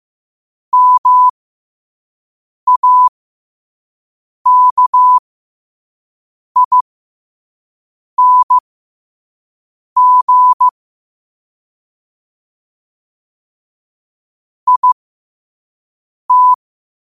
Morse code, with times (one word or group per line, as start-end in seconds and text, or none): 0.7-10.7 MAKING
14.7-16.5 IT